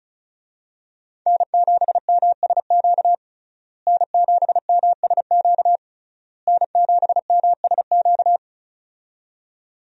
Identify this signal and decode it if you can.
Morse code, heard as a keyed tone, about 36 wpm, 700 Hz. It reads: N7MSQ N7MSQ N7MSQ
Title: USA Call Signs - 5 Characters - Repeated 3x - 35wpm